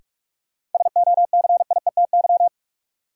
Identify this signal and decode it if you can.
Morse code, heard as a keyed tone, about 46 wpm, 700 Hz. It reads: SOCIETY
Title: Sentences from Top 700 Words - New Word Drill - 45wpm